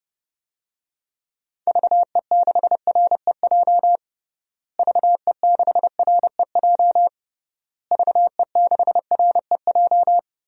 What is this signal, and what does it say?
Morse code, heard as a keyed tone, about 32 wpm, 700 Hz. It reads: VE6REJ VE6REJ VE6REJ
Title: DXCC-100 Call Signs - 6 Characters - Repeated 3x - 30wpm